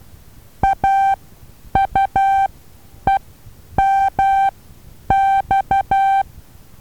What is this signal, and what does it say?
Morse code, heard as a keyed tone, about 12 wpm, spaced as for 9 wpm, 780 Hz. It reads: AUEMX
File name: Taviro HA6VA B